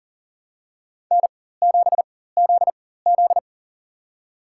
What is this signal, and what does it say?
Morse code, heard as a keyed tone, about 40 wpm, spaced as for 19 wpm, 700 Hz. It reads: N7ZZ
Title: USA Call Signs - 4 Characters - 4x Character Spacing - 40wpm